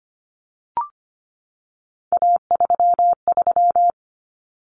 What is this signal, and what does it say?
Morse code, heard as a keyed tone, about 25 wpm, 700 Hz. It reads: A33
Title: Common Cut Numbers - 3-Digit 25wpm